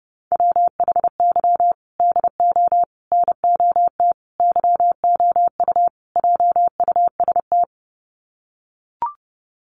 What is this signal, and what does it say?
Morse code, heard as a keyed tone, about 30 wpm, 700 Hz. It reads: WHY DO NOT YOU JUST
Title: Sets of 5 Words 30wpm